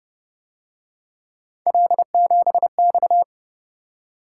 Morse code, heard as a keyed tone, about 29 wpm, 700 Hz. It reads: L7X
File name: Speed-Racing - Three Letter-Number Combinations - 25wpm